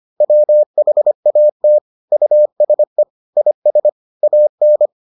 Morse code, heard as a keyed tone, about 25 wpm, 600 Hz. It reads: WHAT USE IS AN